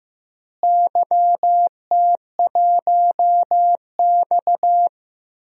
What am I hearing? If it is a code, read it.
Morse code, heard as a keyed tone, about 16 wpm, 700 Hz. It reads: YT1X
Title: DXCC-100 Call Signs - 4 Characters - Repeated 3x - 15wpm